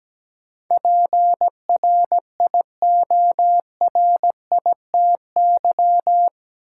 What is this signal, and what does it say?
Morse code, heard as a keyed tone, about 17 wpm, 700 Hz. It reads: PRIORITY
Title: Sentences from Top 1600 Words - New Word Drill - 17wpm